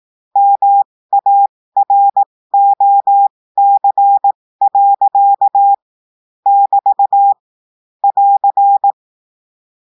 Morse code, heard as a keyed tone, about 18 wpm, spaced as for 15 wpm, 800 Hz. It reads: MAROC. <BT> <AR>
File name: cours-du-lundi files / cours-1580-texte-clair-15-mpm